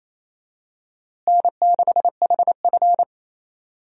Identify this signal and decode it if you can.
Morse code, heard as a keyed tone, about 27 wpm, 700 Hz. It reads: N6HF
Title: USA Call Signs - 4 Characters - Repeated 2x - 28wpm